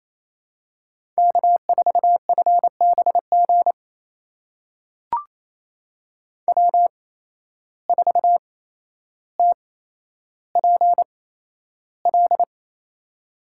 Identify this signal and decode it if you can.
Morse code, heard as a keyed tone, about 28 wpm, spaced as for 12 wpm, 700 Hz. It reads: K4FBG W4TPL
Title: USA Call Signs - 8x Character Spacing - 28wpm